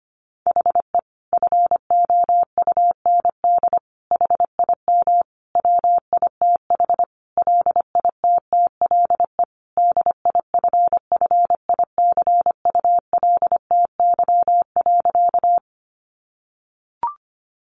Morse code, heard as a keyed tone, about 25 wpm, 700 Hz. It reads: HE FOUND HIM WITH LITTLE DIFFICULTY.
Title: Sentences from Top 1500 Words 25wpm